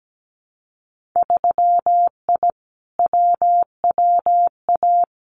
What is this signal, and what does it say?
Morse code, heard as a keyed tone, about 17 wpm, 700 Hz. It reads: 3I WWA
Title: Winter Field Day Exchanges Encore 17wpm